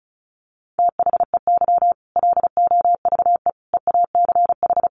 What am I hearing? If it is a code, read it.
Morse code, heard as a keyed tone, about 34 wpm, 700 Hz. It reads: THEY LOVE EACH
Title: Easy Phrases 35wpm